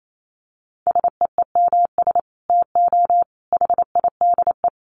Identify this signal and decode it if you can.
Morse code, heard as a keyed tone, about 28 wpm, 700 Hz. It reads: SEEMS TO HIDE
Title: Dit Challenge 28wpm